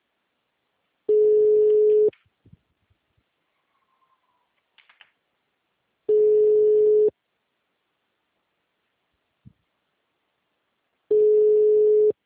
Ringback tone (Europe, 425 Hz).
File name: out- -2011- - - .50